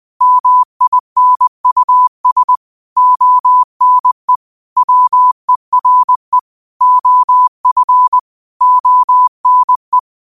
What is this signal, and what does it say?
Morse code, heard as a keyed tone, about 20 wpm, 1000 Hz. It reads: MINUS ONE WERE OF ONE
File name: qotdfiles / 20wpm